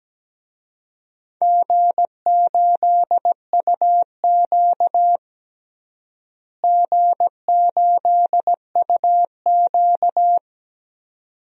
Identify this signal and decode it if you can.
Morse code, heard as a keyed tone, about 17 wpm, 700 Hz. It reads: G8UQ G8UQ
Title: DXCC-100 Call Signs - 4 Characters - Repeated 2x - 17wpm